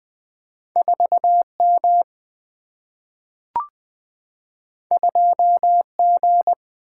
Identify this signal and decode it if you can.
Morse code, heard as a keyed tone, about 20 wpm, 700 Hz. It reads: 4 M 2 G
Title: Single-Letter Single-Number 20wpm